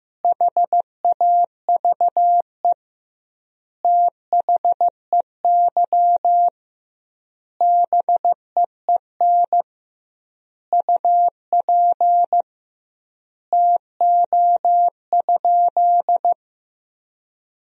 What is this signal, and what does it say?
Morse code, heard as a keyed tone, about 16 wpm, 700 Hz. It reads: HAVE THEY BEEN UP TO?
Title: 2x Word-Spacing - Sentences from Top 100 Words - 15wpm